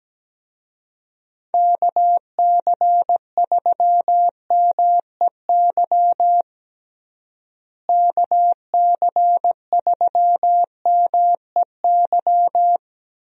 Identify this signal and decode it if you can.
Morse code, heard as a keyed tone, about 17 wpm, 700 Hz. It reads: KC3MEY KC3MEY
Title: USA Call Signs - 6 Characters - Repeated 3x - 17wpm